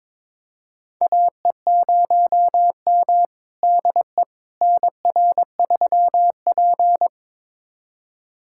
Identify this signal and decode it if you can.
Morse code, heard as a keyed tone, about 22 wpm, 700 Hz. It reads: AE0M DE NR3P
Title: USA Callsign DE Callsign - 4 Characters - 22wpm